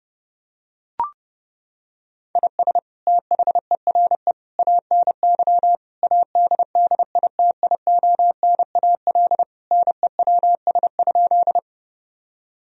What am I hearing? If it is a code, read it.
Morse code, heard as a keyed tone, about 30 wpm, 700 Hz. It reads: IS THERE ANY ADDITIONAL NEWS?
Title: Sentences from Top 1000 Words - Review Part 4 - 30wpm